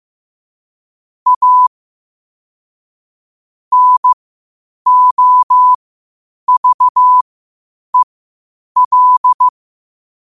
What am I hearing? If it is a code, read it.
Morse code, heard as a keyed tone, about 15 wpm, spaced as for 8 wpm, 1000 Hz. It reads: A NOVEL
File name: qotdfiles / 8wpm